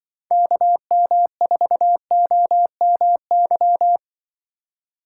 Morse code, heard as a keyed tone, about 25 wpm, 700 Hz. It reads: KM4OMY
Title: Speed Racing - USA 2x3 Call Signs 20wpm